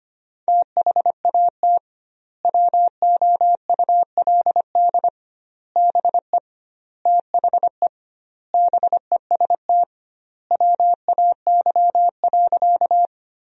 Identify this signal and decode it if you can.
Morse code, heard as a keyed tone, about 25 wpm, 700 Hz. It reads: THAT WOULD BE THE BEST WAY.
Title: 2x Word-Spacing - Sentences from Top 400 Words - 25wpm